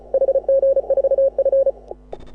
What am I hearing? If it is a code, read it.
Morse code, heard as a keyed tone, about 34 wpm, 530 Hz. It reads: HG4F
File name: AUDIO2018 / HA6OI CW